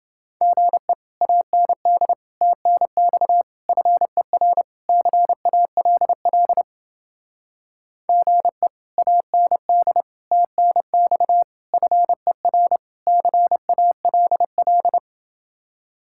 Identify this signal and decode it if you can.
Morse code, heard as a keyed tone, about 28 wpm, 700 Hz. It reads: GE AND TNX FER CALL GE AND TNX FER CALL